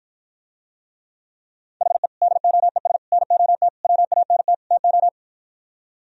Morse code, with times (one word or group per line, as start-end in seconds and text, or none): 1.8-2.1 HE
2.2-3.0 DOES
3.1-3.7 NOT
3.8-4.5 WANT
4.7-5.1 TO